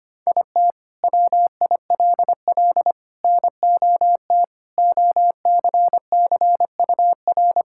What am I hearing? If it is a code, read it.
Morse code, heard as a keyed tone, about 25 wpm, 700 Hz. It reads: IT WILL NOT OCCUR
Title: Sentences from Top 1000 Words - Review Part 6 - 25wpm